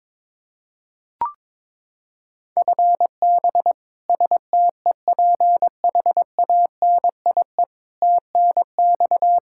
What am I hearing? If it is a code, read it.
Morse code, heard as a keyed tone, about 22 wpm, 700 Hz. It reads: FB STEPHANIE TNX
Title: Practice QSO - Exchange 3 - More Info - 22wpm